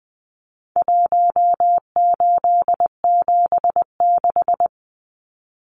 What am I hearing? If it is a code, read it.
Morse code, heard as a keyed tone, about 20 wpm, 700 Hz. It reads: 1876
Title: Numbers - Contest Style - 4-Digit 20wpm